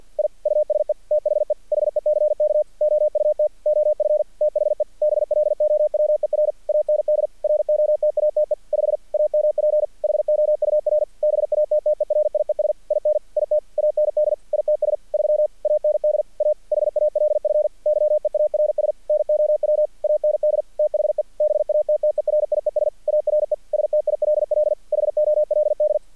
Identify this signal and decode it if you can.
Morse code, heard as a keyed tone, about 48 wpm, 600 Hz. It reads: I USE THE HECK OUT OF THE BLOWER AND ROTATE 5 AMP HOUR BATTERIES IN IT AND ITS 3 AND A HALF YEARS NOW AND THE BATTERIES ARE STILL HOLD
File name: archived / CRSnet